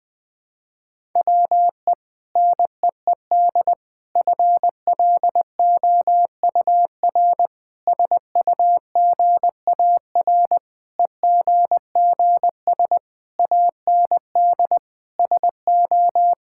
Morse code, heard as a keyed tone, about 20 wpm, 700 Hz. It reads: WE NEED FLOUR SUGAR EGGS AND SO
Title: Sentences with 6-Word Phrases - 20wpm